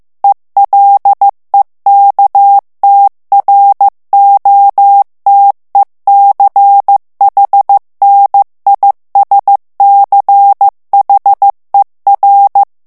Morse code, heard as a keyed tone, about 15 wpm, 800 Hz. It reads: ELEKTROTECHNISCHER
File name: morse mp3 / ALLEMAND texte clair 1 vitesse 14 mots minute